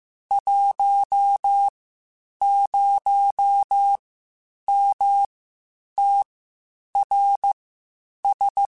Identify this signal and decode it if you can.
Morse code, heard as a keyed tone, about 15 wpm, spaced as for 8 wpm, 800 Hz. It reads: 10MTRS
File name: (spaced) / QSOs Spaced-mp3s / WD0FBT 1 @13 sp